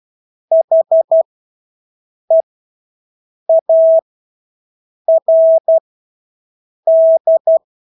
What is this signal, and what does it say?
Morse code, heard as a keyed tone, about 12 wpm, spaced as for 6 wpm, 650 Hz. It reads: HEARD
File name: Engelske ord 12-6 wpm